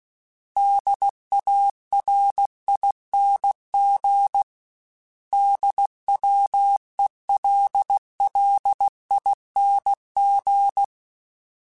Morse code, heard as a keyed tone, about 16 wpm, 800 Hz. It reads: DARING DWELLING
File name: Special INGs / Special INGs-mp3s / Sngl ing @15